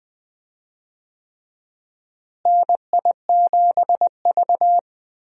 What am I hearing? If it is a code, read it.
Morse code, heard as a keyed tone, about 20 wpm, 700 Hz. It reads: NI7V
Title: USA Call Signs - 4 Characters - Repeated 3x - 20wpm